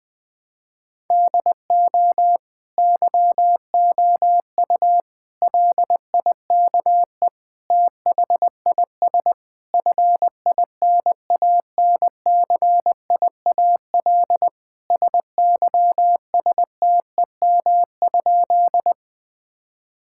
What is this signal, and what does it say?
Morse code, heard as a keyed tone, about 20 wpm, 700 Hz. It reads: DO YOU LIKE THIS FINANCIAL SYSTEM?
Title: Sentences from Top 1000 Words - Review Part 5 - 20wpm